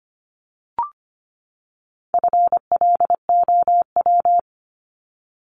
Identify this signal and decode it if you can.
Morse code, heard as a keyed tone, about 25 wpm, 700 Hz. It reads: FLOW
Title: Top 1001-1250 Words 25wpm